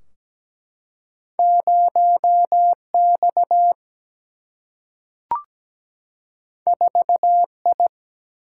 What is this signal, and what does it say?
Morse code, heard as a keyed tone, about 17 wpm, 700 Hz. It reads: 0X 4I
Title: Single-Letter Single-Number 17wpm